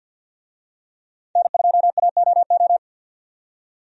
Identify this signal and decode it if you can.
Morse code, heard as a keyed tone, about 52 wpm, 700 Hz. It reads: N2AOO
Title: USA Call Signs - 5 Characters - 50wpm